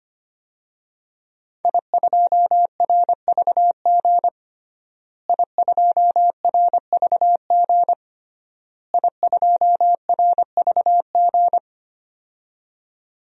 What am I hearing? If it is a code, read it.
Morse code, heard as a keyed tone, about 25 wpm, 700 Hz. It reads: I2RVG I2RVG I2RVG
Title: DXCC-100 Call Signs - 5 Characters - Repeated 3x - 25wpm